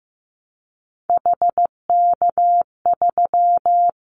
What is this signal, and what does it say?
Morse code, heard as a keyed tone, about 15 wpm, 700 Hz. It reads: HK3
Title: Callsign Prefixes for Top 100 DXCC - 15wpm